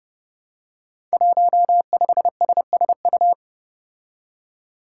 Morse code, heard as a keyed tone, about 30 wpm, 700 Hz. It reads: 15SSU
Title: Five Challenging Letter-Number Combinations 30wpm